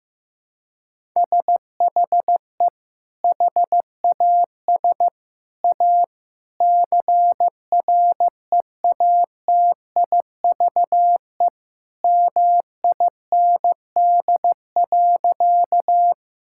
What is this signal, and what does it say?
Morse code, heard as a keyed tone, about 16 wpm, 700 Hz. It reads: SHE HAS A CREATIVE MIND.
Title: Sentences from Top 1000 Words - Review Part 5 - 15wpm